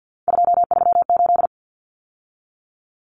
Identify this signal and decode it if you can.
Morse code, heard as a keyed tone, about 50 wpm, 700 Hz. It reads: 238
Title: Numbers - 3-Digit 50wpm